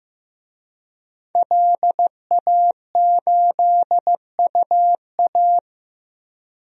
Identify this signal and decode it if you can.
Morse code, heard as a keyed tone, about 15 wpm, 700 Hz. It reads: LA8UA
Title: DXCC-100 Call Signs - 5 Characters - Repeated 3x - 15wpm